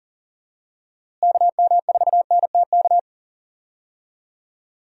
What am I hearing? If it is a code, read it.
Morse code, heard as a keyed tone, about 40 wpm, 700 Hz. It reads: KM4NTK
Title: USA Call Signs - 6 Characters - Repeated 2x - 40wpm